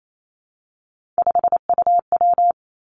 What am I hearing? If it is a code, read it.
Morse code, heard as a keyed tone, about 28 wpm, 700 Hz. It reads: 5UW
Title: Three Challenging Letter-Number Combinations 28wpm